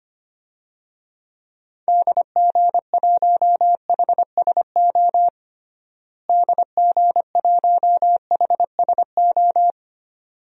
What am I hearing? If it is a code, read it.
Morse code, heard as a keyed tone, about 25 wpm, 700 Hz. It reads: DG1HSO DG1HSO
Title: DXCC-100 Call Signs - 6 Characters - Repeated 2x - 25wpm